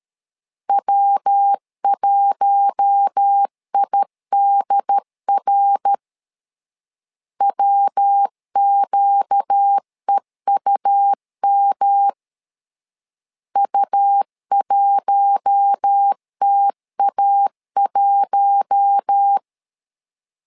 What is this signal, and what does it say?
Morse code, heard as a keyed tone, about 13 wpm, 800 Hz. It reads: W1IDR WQEUM U1TA1
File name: LEZIONI / 4 Q1 12wpm 2028